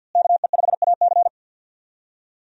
Morse code, heard as a keyed tone, about 48 wpm, 700 Hz. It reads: KE5AC
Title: USA Call Signs - 5 Characters - Repeated 2x - 50wpm